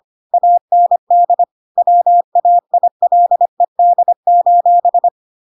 Morse code, heard as a keyed tone, about 25 wpm, 700 Hz. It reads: AND WAILED:
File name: Pinocchio / Pinocchio ch7 25 25 ik1hge